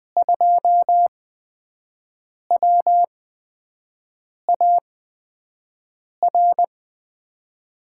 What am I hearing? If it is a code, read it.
Morse code, heard as a keyed tone, about 20 wpm, spaced as for 5 wpm, 700 Hz. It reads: 2WAR
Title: USA Call Signs - 8x Character Spacing - 20wpm